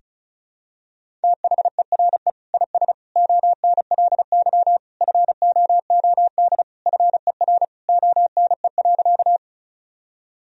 Morse code, heard as a keyed tone, about 36 wpm, 700 Hz. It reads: THERE IS ONLY FOOD FER ONE.
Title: Sentences from Top 300 Words and Abbreviations 35wpm